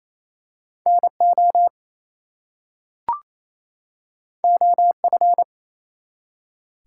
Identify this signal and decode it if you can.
Morse code, heard as a keyed tone, about 27 wpm, 700 Hz. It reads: NO OF
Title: Two Letter Words - 28wpm